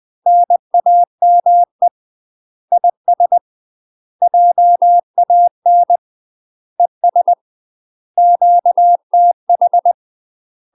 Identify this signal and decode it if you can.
Morse code, heard as a keyed tone, about 20 wpm, 700 Hz. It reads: NAME IS JAN ES QTH
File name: UK MASTER QSOs TRAINING / 20WPM / UK MASTER QSOs 0036